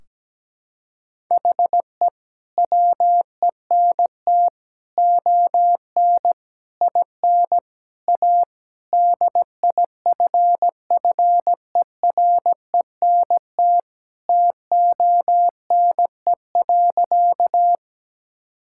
Morse code, heard as a keyed tone, about 17 wpm, 700 Hz. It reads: HE WENT ON IN A DIFFERENT TONE.
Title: Sentences from Top 1600 Words 17wpm